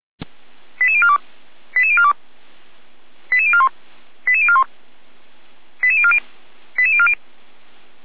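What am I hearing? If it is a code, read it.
Five-tone selective call (ZVEI), heard as quick runs of five tones, tones of about 70 ms. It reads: 88042 88042 88041 88041 88040 88040
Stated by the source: ZVEI five-tone call 88103 received on an SDR, Feuerwehr Orlamünde.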